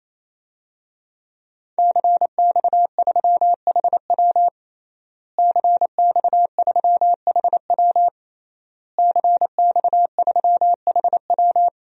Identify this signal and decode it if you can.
Morse code, heard as a keyed tone, about 28 wpm, 700 Hz. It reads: CX3HW CX3HW CX3HW
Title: DXCC-100 Call Signs - 5 Characters - Repeated 3x - 28wpm